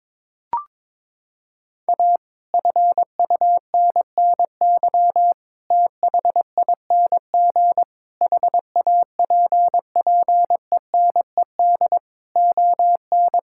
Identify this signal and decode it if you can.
Morse code, heard as a keyed tone, about 22 wpm, 700 Hz. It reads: A FUNNY THING HAPPENED ON
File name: Sets of 9 Words - 22wpm